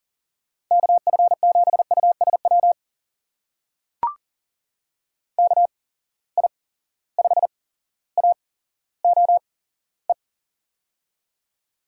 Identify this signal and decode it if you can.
Morse code, heard as a keyed tone, about 40 wpm, spaced as for 17 wpm, 700 Hz. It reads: KF7USW KI5AOE
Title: USA Call Signs - 6 Characters - 8x Character Spacing - 40wpm